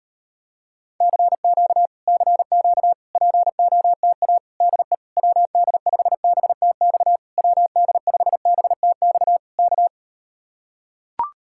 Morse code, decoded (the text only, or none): CQ CQ POTA DE WD5BTX WD5BTX K